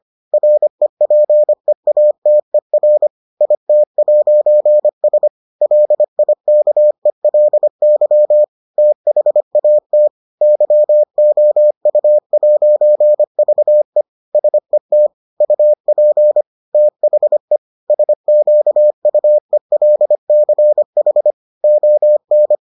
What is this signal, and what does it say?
Morse code, heard as a keyed tone, about 25 wpm, 600 Hz. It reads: REPEATER IT'S LIKELY THAT YOU'VE SET UP THE SQUELCH ON